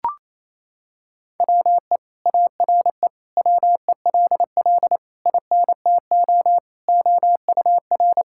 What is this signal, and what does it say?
Morse code, heard as a keyed tone, about 27 wpm, 700 Hz. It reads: WE ARE WELL INTO OUR
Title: Sentences from Top 1000 Words - Review Part 5 - 28wpm